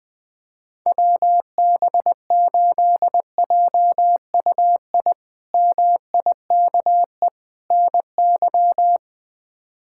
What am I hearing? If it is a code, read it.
Morse code, heard as a keyed tone, about 21 wpm, 700 Hz. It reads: WB8JUI MIKE NY
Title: K1USN SST QSO Practice - 20wpm